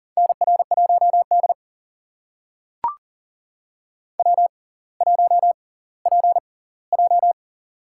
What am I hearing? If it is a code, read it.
Morse code, heard as a keyed tone, about 38 wpm, spaced as for 20 wpm, 700 Hz. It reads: NR1D W1PJ
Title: USA Call Signs - 4 Characters - 6x Character Spacing - 40wpm